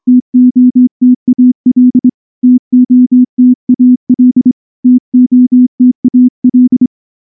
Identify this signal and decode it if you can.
Morse code, heard as a keyed tone, about 24 wpm, 250 Hz. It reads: TOTAL TOTAL TOTAL
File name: morse / morse-training-25WPM